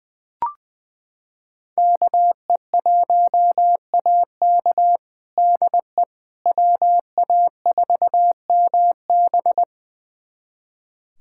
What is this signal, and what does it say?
Morse code, heard as a keyed tone, about 21 wpm, 700 Hz. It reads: KE1AK DE WA4MB
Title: USA Callsign DE Callsign - 5 Characters - 20wpm